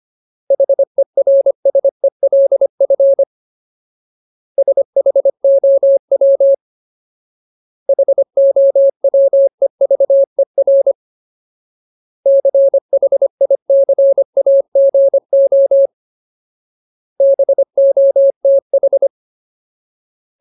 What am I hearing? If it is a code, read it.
Morse code, heard as a keyed tone, about 25 wpm, 550 Hz. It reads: HERSELF SHOW HOWEVER CHICAGO BOTH
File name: Ultimate Word List / 25WPM Ultimate